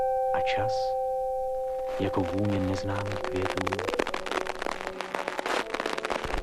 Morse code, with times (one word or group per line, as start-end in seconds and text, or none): none